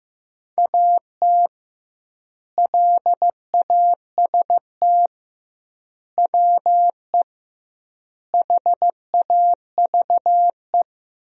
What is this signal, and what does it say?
Morse code, heard as a keyed tone, about 15 wpm, 700 Hz. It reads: AT LAST WE HAVE
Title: 2x Word-Spacing - Sentences from Top 300 Words - 15wpm